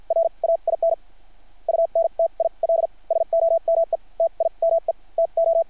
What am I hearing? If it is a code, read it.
Morse code, heard as a keyed tone, about 50 wpm, spaced as for 39 wpm, 660 Hz. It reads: RAIN UNTIL SOMETIME TO
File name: archived / CRSnet